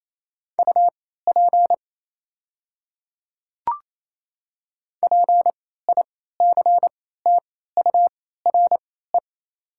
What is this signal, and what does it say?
Morse code, heard as a keyed tone, about 28 wpm, spaced as for 16 wpm, 700 Hz. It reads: UP PICTURE
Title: 3x Character Spacing - Top 300 Words - 28wpm